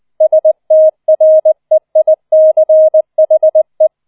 Morse code, heard as a keyed tone, about 20 wpm, 630 Hz. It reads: STREICHE
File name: 004 Vierter Streich 20wpm